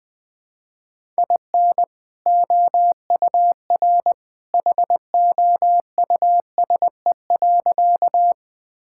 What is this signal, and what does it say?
Morse code, heard as a keyed tone, about 20 wpm, 700 Hz. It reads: IN OUR HOUSE.
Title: Sentences with 2-Word Phrases - Spoken Beginning - 20wpm